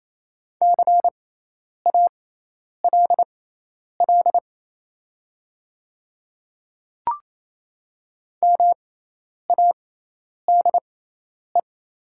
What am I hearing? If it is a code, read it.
Morse code, heard as a keyed tone, about 28 wpm, spaced as for 10 wpm, 690 Hz. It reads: CALL MADE
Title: 6x Character Spacing - Top 200 Words - 28wpm